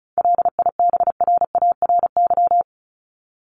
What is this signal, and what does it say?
Morse code, heard as a keyed tone, about 34 wpm, 700 Hz. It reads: LIBRARY
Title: Top 501-750 Words 35wpm